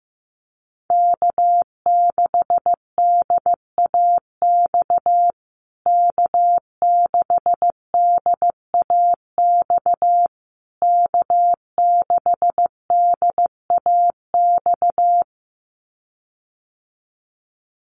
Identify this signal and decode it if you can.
Morse code, heard as a keyed tone, about 15 wpm, 700 Hz. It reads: K6DAX K6DAX K6DAX
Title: USA Call Signs Repeated 3x 15wpm